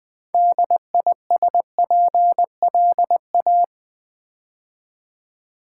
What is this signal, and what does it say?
Morse code, heard as a keyed tone, about 21 wpm, 700 Hz. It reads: DISPLA
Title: Build A Word - 3 Letters At A Time - 8-Letter Words - 20wpm